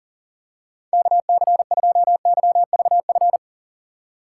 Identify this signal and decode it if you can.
Morse code, heard as a keyed tone, about 40 wpm, 700 Hz. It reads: KC2YVF
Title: USA Call Signs - 6x Character Spacing - 40wpm